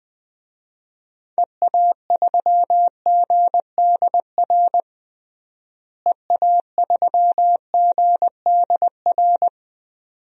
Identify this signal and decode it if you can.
Morse code, heard as a keyed tone, about 20 wpm, 700 Hz. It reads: EA3GDR EA3GDR
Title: DXCC-100 Call Signs - 6 Characters - Repeated 2x - 20wpm